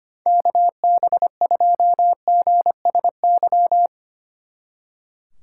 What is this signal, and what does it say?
Morse code, heard as a keyed tone, about 33 wpm, 700 Hz. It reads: KB2GSY